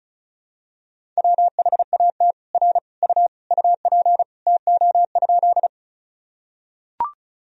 Morse code, heard as a keyed tone, about 34 wpm, 700 Hz. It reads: WHAT R U UP TO?
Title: Sentences from Top 100 Words and Abbreviations 35wpm